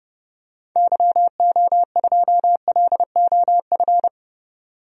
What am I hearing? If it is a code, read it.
Morse code, heard as a keyed tone, about 32 wpm, 700 Hz. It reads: YO2LOF
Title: DXCC-100 Call Signs - 6 Characters - 30wpm